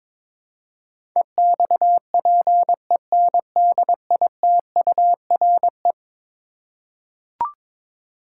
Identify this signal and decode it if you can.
Morse code, heard as a keyed tone, about 22 wpm, 700 Hz. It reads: EXPENDITURE
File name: practice / 22wpm / Eleven Letter Words - 22wpm